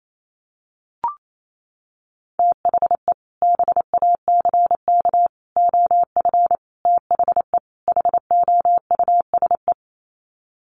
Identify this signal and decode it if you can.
Morse code, heard as a keyed tone, about 28 wpm, 700 Hz. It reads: THE BACK OF THE HOUSE
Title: Sets of 5 Words Encore2 28wpm